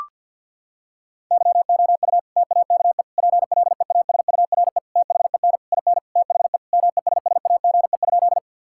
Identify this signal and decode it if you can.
Morse code, heard as a keyed tone, about 50 wpm, 700 Hz. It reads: YOU TAKE PLEASURE THEN IN THE MESSAGE?
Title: Sentences from Top 1600 Words 50wpm